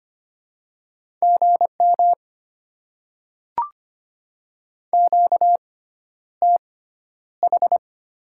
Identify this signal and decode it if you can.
Morse code, heard as a keyed tone, about 25 wpm, spaced as for 11 wpm, 700 Hz. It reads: GM QTH
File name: Top 20 QSO Elements - 6x Character Spacing - 25wpm